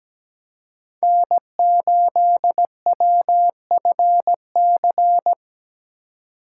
Morse code, heard as a keyed tone, about 17 wpm, 700 Hz. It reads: N8WFC
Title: USA Call Signs - 5 Characters - 2x Character Spacing - 17wpm